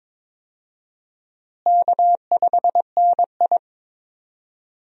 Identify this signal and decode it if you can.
Morse code, heard as a keyed tone, about 22 wpm, 700 Hz. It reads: K5NI
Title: USA Call Signs - 4 Characters - Repeated 3x - 22wpm